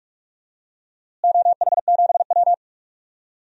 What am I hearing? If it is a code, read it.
Morse code, heard as a keyed tone, about 46 wpm, 700 Hz. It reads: OH7W
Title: DXCC-100 Call Signs - 4 Characters - Repeated 2x - 45wpm